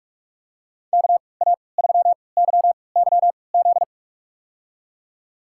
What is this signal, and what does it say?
Morse code, heard as a keyed tone, about 43 wpm, spaced as for 25 wpm, 700 Hz. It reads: KA3YYZ
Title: USA Call Signs - 3x Character Spacing - 45wpm